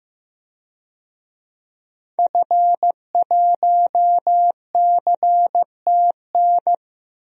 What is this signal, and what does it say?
Morse code, heard as a keyed tone, about 15 wpm, 700 Hz. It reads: F1CTN
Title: DXCC-100 Call Signs - 5 Characters - 15wpm